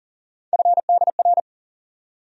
Morse code, heard as a keyed tone, about 40 wpm, 700 Hz. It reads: FDR
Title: USA Call Sign Postfixes - 3 Letters 40wpm